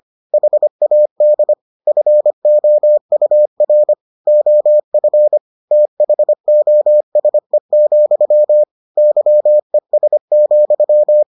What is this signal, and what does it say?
Morse code, heard as a keyed tone, about 25 wpm, 600 Hz. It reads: HAD FOUR OF THOSE, YES,